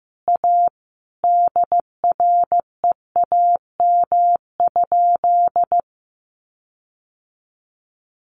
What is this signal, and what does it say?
Morse code, heard as a keyed tone, about 14 wpm, 700 Hz. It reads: A DREAM?
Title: Sentences from Top 1500 Words 15wpm